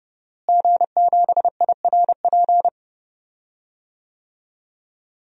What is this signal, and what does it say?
Morse code, heard as a keyed tone, about 30 wpm, 700 Hz. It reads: G7IRP
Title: DXCC-100 Call Signs - 5 Characters - Repeated 3x - 30wpm